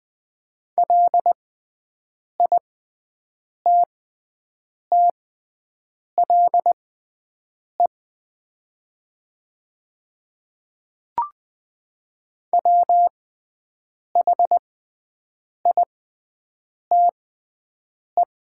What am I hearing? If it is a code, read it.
Morse code, heard as a keyed tone, about 20 wpm, spaced as for 7 wpm, 700 Hz. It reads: LITTLE WHITE
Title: 6x Character Spacing - Top 500 Words - 20wpm